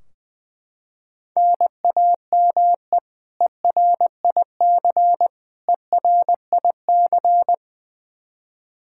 Morse code, heard as a keyed tone, about 20 wpm, 700 Hz. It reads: NAME ERIC ERIC
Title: Practice QSO - Exchange 1 - Beginning - 20wpm